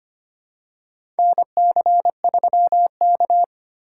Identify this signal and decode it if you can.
Morse code, heard as a keyed tone, about 25 wpm, 700 Hz. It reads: NC3K
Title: USA Call Signs - 4 Characters - 4x Character Spacing - 25wpm